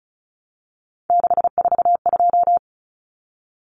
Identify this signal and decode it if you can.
Morse code, heard as a keyed tone, about 34 wpm, 700 Hz. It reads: 642
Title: Numbers - 3-Digit 35wpm